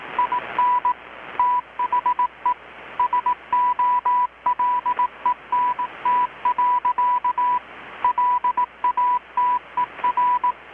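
Morse code, heard as a keyed tone, about 18 wpm, 990 Hz. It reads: IN THE SOLENT. LATER
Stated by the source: Morse from KPH's Night of Nights XX maritime radiotelegraphy event, received in Chiapas, Mexico about 3,800 km away on a Kenwood R-600 with a grounded 10 m long-wire antenna.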